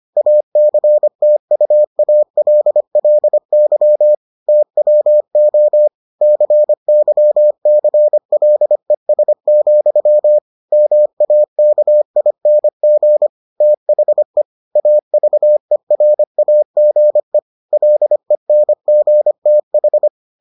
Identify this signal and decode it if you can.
Morse code, heard as a keyed tone, about 25 wpm, 600 Hz. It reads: ACTUALLY TWO CYCLES, MAKING THE AVERAGE LENGTH